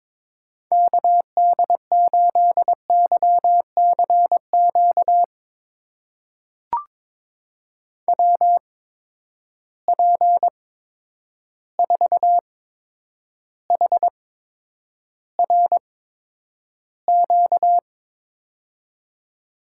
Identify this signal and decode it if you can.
Morse code, heard as a keyed tone, about 22 wpm, spaced as for 9 wpm, 700 Hz. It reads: KD8YCQ WP4HRQ
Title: USA Call Signs - 6 Characters - 8x Character Spacing - 22wpm